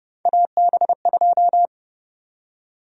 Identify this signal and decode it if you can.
Morse code, heard as a keyed tone, about 30 wpm, 700 Hz. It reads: AB2
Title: USA Call Sign Prefixes - Repeated 2x - 30wpm